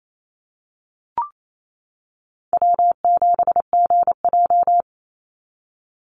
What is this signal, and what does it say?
Morse code, heard as a keyed tone, about 28 wpm, 700 Hz. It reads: W7GJ
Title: NA SOTA Chaser Callsigns 28wpm